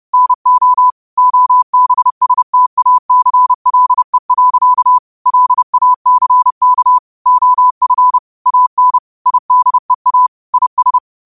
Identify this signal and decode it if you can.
Morse code, heard as a keyed tone, about 30 wpm, 1000 Hz. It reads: NO OBSTACLE. LACK OF AN IDEA IS